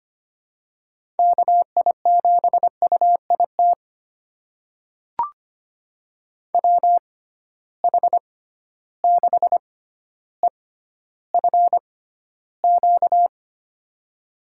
Morse code, heard as a keyed tone, about 25 wpm, spaced as for 13 wpm, 700 Hz. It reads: KI7UIT WH6EFQ
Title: USA Call Signs - 6 Characters - 6x Character Spacing - 25wpm